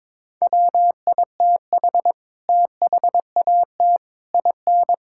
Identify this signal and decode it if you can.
Morse code, heard as a keyed tone, about 22 wpm, 700 Hz. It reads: WITH THAT IN
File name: Sets of 9 Words - 22wpm